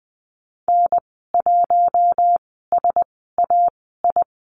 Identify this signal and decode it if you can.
Morse code, heard as a keyed tone, about 20 wpm, spaced as for 15 wpm, 700 Hz. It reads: N1SAI
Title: 048 Lesson 9g - Number 4 - Call Signs 20wpm